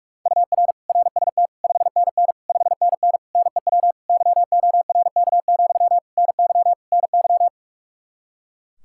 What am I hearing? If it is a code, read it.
Morse code, heard as a keyed tone, about 46 wpm, 700 Hz. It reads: UR RST 5NN 5NN NEW YORK, NY NY
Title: Practice QSO - Exchange 1 - Beginning - 45wpm